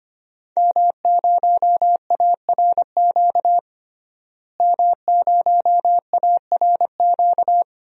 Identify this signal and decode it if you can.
Morse code, heard as a keyed tone, about 25 wpm, 700 Hz. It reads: M0ARQ M0ARQ
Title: DXCC-100 Call Signs - 5 Characters - Repeated 2x - 25wpm